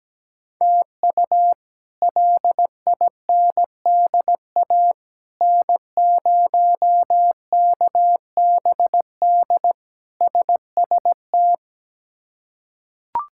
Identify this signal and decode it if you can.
Morse code, heard as a keyed tone, about 17 wpm, 700 Hz. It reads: TU LINDA N0KBD SST
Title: K1USN SST QSO Practice - 17wpm